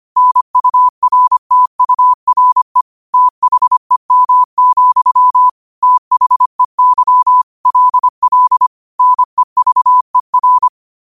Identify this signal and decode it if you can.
Morse code, heard as a keyed tone, about 25 wpm, 1000 Hz. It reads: NURTURE THEM, THEY LL NEVER
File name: qotdfiles / 25wpm